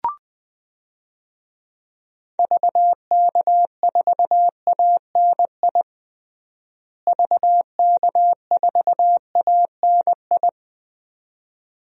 Morse code, heard as a keyed tone, about 21 wpm, 700 Hz. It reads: VK4ANI VK4ANI
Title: DXCC-100 Call Signs - 6 Characters - Repeated 2x - 20wpm